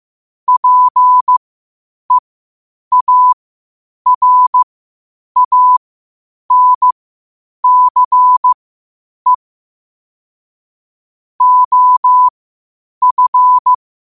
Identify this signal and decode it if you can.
Morse code, heard as a keyed tone, about 15 wpm, spaced as for 8 wpm, 1000 Hz. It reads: PEARANCE OF
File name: 8wpm